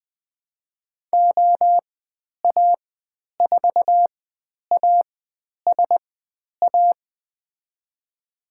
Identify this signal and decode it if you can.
Morse code, heard as a keyed tone, about 20 wpm, spaced as for 10 wpm, 700 Hz. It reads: OA4ASA
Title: 102 Lesson 17g - Letter u - Call Signs 20wpm